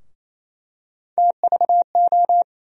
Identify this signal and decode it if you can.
Morse code, heard as a keyed tone, about 28 wpm, 700 Hz. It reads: TVO
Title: USA Call Sign Postfixes - 3 Letters 28wpm